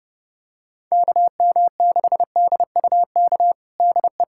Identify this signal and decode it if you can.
Morse code, heard as a keyed tone, about 30 wpm, 690 Hz. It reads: KM6DUK DE
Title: USA Callsign DE Callsign - 6 Characters - 30wpm